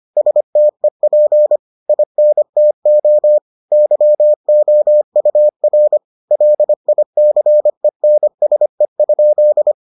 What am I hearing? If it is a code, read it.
Morse code, heard as a keyed tone, about 25 wpm, 600 Hz. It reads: STEP INTO YOUR LICENSE?